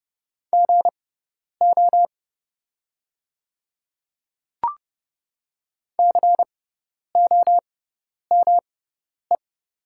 Morse code, heard as a keyed tone, about 32 wpm, spaced as for 10 wpm, 700 Hz. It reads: GO COME
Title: 6x Character Spacing - Top 100 Words - 30wpm